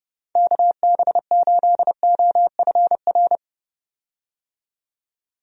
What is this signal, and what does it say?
Morse code, heard as a keyed tone, about 30 wpm, 700 Hz. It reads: KB8OFR
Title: USA Call Signs - 6 Characters - 30wpm